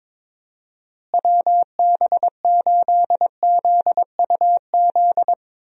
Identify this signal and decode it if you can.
Morse code, heard as a keyed tone, about 21 wpm, 700 Hz. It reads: WB8ZUZ
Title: USA Call Signs - 6 Characters - Repeated 2x - 22wpm